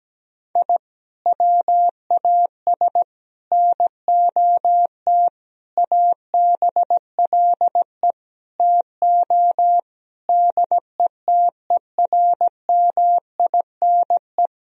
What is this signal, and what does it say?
Morse code, heard as a keyed tone, about 17 wpm, 700 Hz. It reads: I WAS NOT ABLE TO DETERMINE
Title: Sentences from Top 1000 Words - Review Part 4 - 17wpm